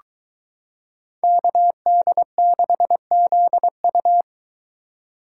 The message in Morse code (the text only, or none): KD6ZU